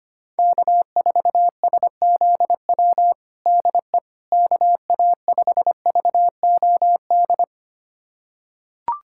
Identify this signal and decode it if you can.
Morse code, heard as a keyed tone, about 25 wpm, 700 Hz. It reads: K4SZW DE KA5VOD
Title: USA Callsign DE Callsign 25wpm